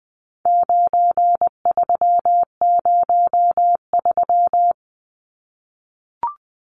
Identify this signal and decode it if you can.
Morse code, heard as a keyed tone, about 20 wpm, 700 Hz. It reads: 9303
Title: Numbers - 4-Digit 20wpm